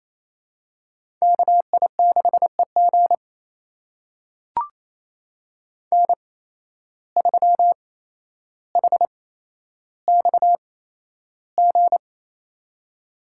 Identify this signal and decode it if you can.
Morse code, heard as a keyed tone, about 28 wpm, spaced as for 12 wpm, 700 Hz. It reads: KI6EG N3HXG
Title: USA Call Signs - 5 Characters - 8x Character Spacing - 28wpm